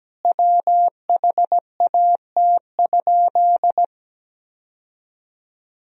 Morse code, heard as a keyed tone, about 17 wpm, 700 Hz. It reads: WHAT?
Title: 4x Word-Spacing - Sentences from Top 100 Words - 17wpm